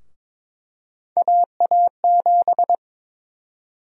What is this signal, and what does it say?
Morse code, heard as a keyed tone, about 22 wpm, 700 Hz. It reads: AA7
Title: USA Call Sign Prefixes - Repeated 2x - 22wpm